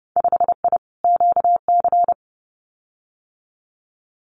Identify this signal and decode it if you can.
Morse code, heard as a keyed tone, about 30 wpm, 700 Hz. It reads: 5I QC
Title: Winter Field Day Exchanges Encore 30wpm